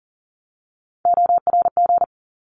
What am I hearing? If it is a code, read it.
Morse code, heard as a keyed tone, about 40 wpm, 700 Hz. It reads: ORG
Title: ARRL Sections 40wpm